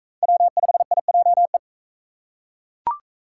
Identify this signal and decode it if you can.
Morse code, heard as a keyed tone, about 40 wpm, 700 Hz. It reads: W5IJE